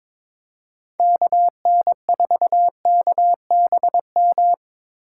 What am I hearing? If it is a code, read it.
Morse code, heard as a keyed tone, about 23 wpm, 700 Hz. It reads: KN4KBM
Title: USA Call Signs - 2x Character Spacing - 22wpm